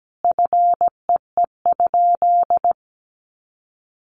Morse code, heard as a keyed tone, about 17 wpm, 700 Hz. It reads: FEE?
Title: Sentences from Top 1500 Words 17wpm